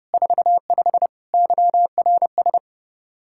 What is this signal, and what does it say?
Morse code, heard as a keyed tone, about 38 wpm, 700 Hz. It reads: 45 YRS